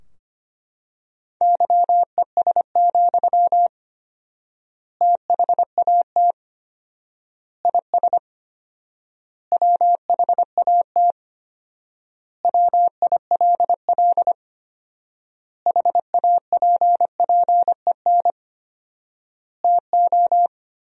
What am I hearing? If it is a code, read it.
Morse code, heard as a keyed tone, about 25 wpm, 700 Hz. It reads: YES, THAT IS WHAT WILL HAPPEN TO
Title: 4x Word-Spacing - Sentences from Top 500 Words - 25wpm